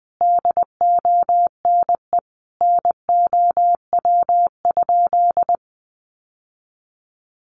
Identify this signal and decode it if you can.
Morse code, heard as a keyed tone, about 20 wpm, 700 Hz. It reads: DONE NOW?
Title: Sentences from Top 1100 Words 20wpm